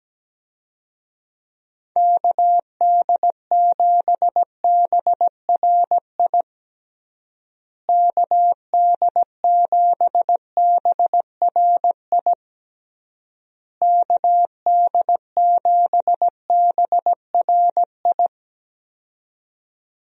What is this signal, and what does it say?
Morse code, heard as a keyed tone, about 17 wpm, 700 Hz. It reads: KD7BRI KD7BRI KD7BRI
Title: USA Call Signs - 6 Characters - Repeated 3x - 17wpm